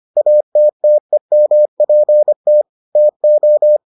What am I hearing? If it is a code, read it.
Morse code, heard as a keyed tone, about 25 wpm, 600 Hz. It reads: ATTEMPT TO